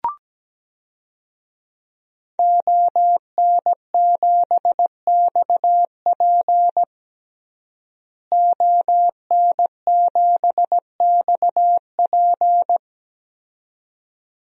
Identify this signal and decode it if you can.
Morse code, heard as a keyed tone, about 17 wpm, 700 Hz. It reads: ON7XP ON7XP
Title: DXCC-100 Call Signs - 5 Characters - Repeated 2x - 17wpm